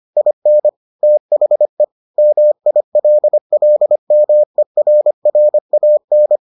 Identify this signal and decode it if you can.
Morse code, heard as a keyed tone, about 25 wpm, 600 Hz. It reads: IN THE MILLMERRAN